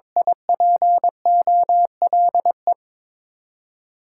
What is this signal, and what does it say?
Morse code, heard as a keyed tone, about 22 wpm, 700 Hz. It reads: I P O L E
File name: Top 90 QSO Elements - 6x Character Spacing - 22wpm